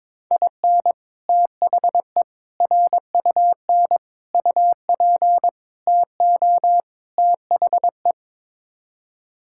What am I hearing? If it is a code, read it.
Morse code, heard as a keyed tone, about 22 wpm, 700 Hz. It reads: IN THE RUN UP TO THE
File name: practice / 22wpm / Sets of 6 Words - 22wpm